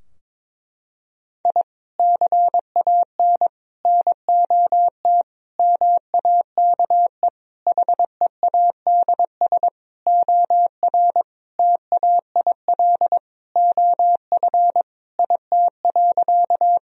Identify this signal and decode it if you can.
Morse code, heard as a keyed tone, about 22 wpm, 700 Hz. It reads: I CAN NOT MAKE HEADS OR TAIL OF IT.
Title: Sentences from Top 1000 Words - Review Part 3 - 22wpm